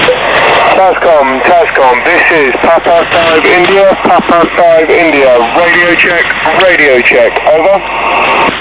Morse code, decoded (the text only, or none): none